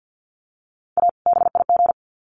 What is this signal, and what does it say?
Morse code, heard as a keyed tone, about 50 wpm, 700 Hz. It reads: A BIG
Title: Sets of 2 Words 50wpm